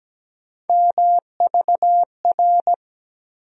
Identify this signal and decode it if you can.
Morse code, heard as a keyed tone, about 17 wpm, 700 Hz. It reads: MVR
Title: USA Call Sign Postfixes - 3 Letters 17wpm